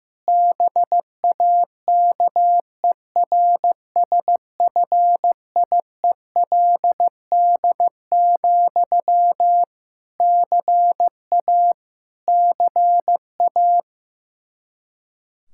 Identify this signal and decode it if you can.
Morse code, heard as a keyed tone, about 15 wpm, 700 Hz. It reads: BAKERSFIELD, CA CA
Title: Practice QSO - Exchange 1 - Beginning - 15wpm